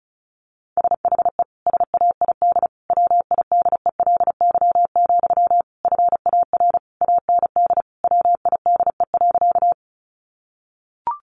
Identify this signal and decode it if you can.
Morse code, heard as a keyed tone, about 34 wpm, 700 Hz. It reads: SHE SAID WIDELY, FAR AND WIDE.
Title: Sentences from Top 1200 Words 35wpm